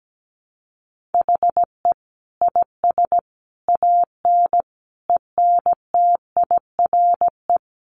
Morse code, heard as a keyed tone, about 17 wpm, 700 Hz. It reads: HE IS AN ENTIRE
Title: A Sound Of Thunder 17wpm